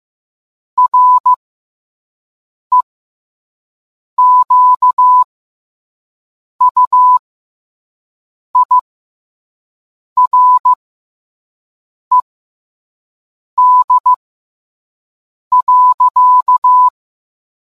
Morse code, heard as a keyed tone, about 15 wpm, spaced as for 5 wpm, 1000 Hz. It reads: REQUIRED.